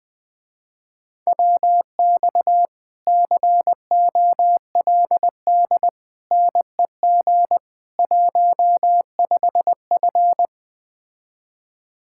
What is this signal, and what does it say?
Morse code, heard as a keyed tone, about 20 wpm, 700 Hz. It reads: WX COLD NEG 15F
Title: Speed-Racing Common QSO Phrases - 17wpm